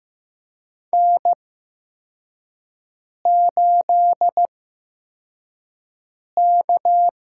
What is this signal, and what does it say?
Morse code, heard as a keyed tone, about 15 wpm, spaced as for 4 wpm, 700 Hz. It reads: N8K
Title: USA Call Signs - 5 Characters - 8x Character Spacing - 15wpm